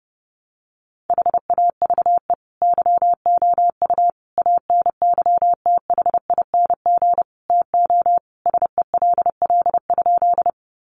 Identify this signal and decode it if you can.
Morse code, heard as a keyed tone, about 30 wpm, 700 Hz. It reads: HAVE YOU ANYTHING TO SELL?
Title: Sentences from Top 1100 Words 30wpm